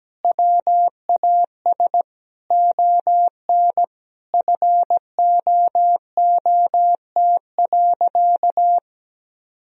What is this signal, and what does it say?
Morse code, heard as a keyed tone, about 17 wpm, 700 Hz. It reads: WAS ON FOOT.
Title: Sentences from Top 1000 Words - Review Part 3 - 17wpm